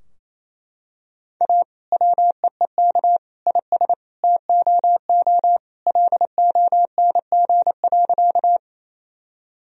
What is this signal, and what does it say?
Morse code, heard as a keyed tone, about 28 wpm, 700 Hz. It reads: A WEEK IS TOO LONG.
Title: Sentences from Top 1000 Words - Review Part 2 - 28wpm